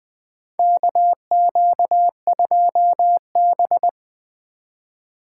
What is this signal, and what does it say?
Morse code, heard as a keyed tone, about 20 wpm, 700 Hz. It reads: KQ2B
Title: USA Call Signs - 4 Characters - 2x Character Spacing - 20wpm